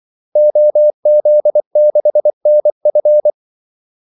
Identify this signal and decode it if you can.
Morse code, heard as a keyed tone, about 24 wpm, 600 Hz. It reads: OZ6NF
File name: Callsigns-24wpm